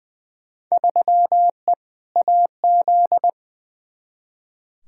Morse code, heard as a keyed tone, about 21 wpm, 700 Hz. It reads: 3E AZ
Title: Field Day Exchanges - 20wpm